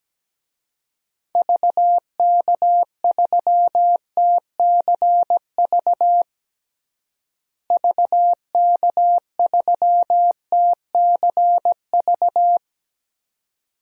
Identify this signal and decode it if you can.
Morse code, heard as a keyed tone, about 17 wpm, 700 Hz. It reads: VK3TCV VK3TCV
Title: DXCC-100 Call Signs - 6 Characters - Repeated 2x - 17wpm